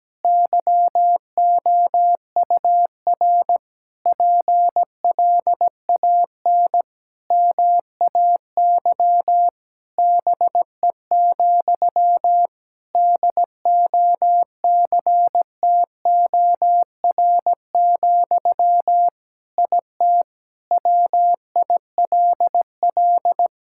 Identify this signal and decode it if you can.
Morse code, heard as a keyed tone, about 17 wpm, 700 Hz. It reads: YOUR PLAN MAY BE, DOCTOR, IT WILL
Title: Sentences from Top 1600 Words 17wpm